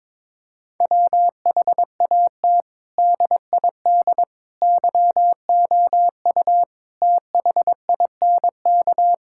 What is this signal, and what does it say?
Morse code, heard as a keyed tone, about 22 wpm, 700 Hz. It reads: WHAT DID YOU THINK
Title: Sentences from Top 1000 Words - Review Part 5 - 22wpm